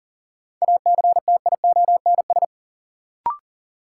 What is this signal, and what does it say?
Morse code, heard as a keyed tone, about 37 wpm, 700 Hz. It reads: ACTIONS
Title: Sentences from Top 1000 Words - Review Part 6 - New Word Drill - 40wpm